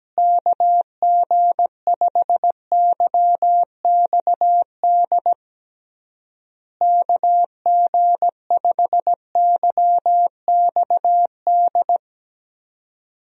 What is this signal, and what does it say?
Morse code, heard as a keyed tone, about 17 wpm, 700 Hz. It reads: KG5YXD KG5YXD
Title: USA Call Signs - 6 Characters - Repeated 3x - 17wpm